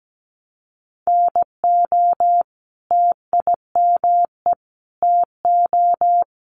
Morse code, heard as a keyed tone, about 17 wpm, 700 Hz. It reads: NO TIME TO
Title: Sentences from Top 1100 Words 17wpm